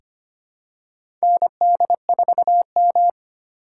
Morse code, heard as a keyed tone, about 25 wpm, 700 Hz. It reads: ND4M
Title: USA Call Signs - 4 Characters - 2x Character Spacing - 25wpm